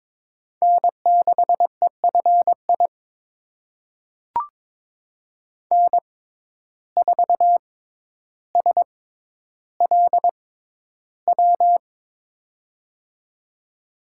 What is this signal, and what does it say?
Morse code, heard as a keyed tone, about 22 wpm, spaced as for 11 wpm, 700 Hz. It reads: N6EFI N4SLW
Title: USA Call Signs - 5 Characters - 6x Character Spacing - 22wpm